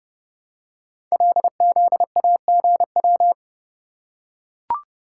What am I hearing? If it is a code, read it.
Morse code, heard as a keyed tone, about 30 wpm, 700 Hz. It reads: LZAGW
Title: Code Groups - 5 Letter:Numbers - 30wpm